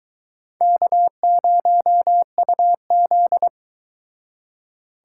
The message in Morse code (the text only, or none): K0UZ